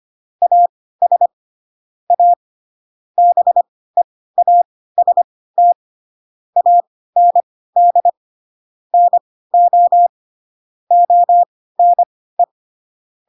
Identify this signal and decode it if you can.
Morse code, heard as a keyed tone, about 25 wpm, spaced as for 16 wpm, 700 Hz. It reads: AS A BEAST AND NO ONE
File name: Pinocchio / Pinocchio ch2 10 25 ik1hge